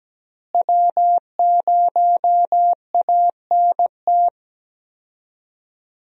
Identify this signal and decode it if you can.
Morse code, heard as a keyed tone, about 17 wpm, 700 Hz. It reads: W0ANT
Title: USA Call Signs - 5 Characters - Repeated 2x - 17wpm